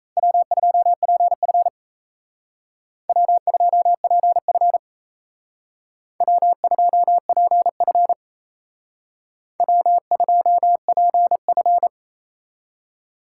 Morse code, heard as a keyed tone, about 35 wpm, 700 Hz. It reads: W2PF W2PF W2PF W2PF
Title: Speed Racing - USA Call Signs 28wpm